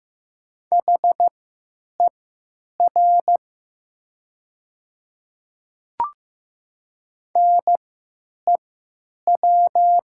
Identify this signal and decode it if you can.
Morse code, heard as a keyed tone, about 15 wpm, spaced as for 9 wpm, 700 Hz. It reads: HER NEW
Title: 3x Character Spacing - Top 200 Words - 15wpm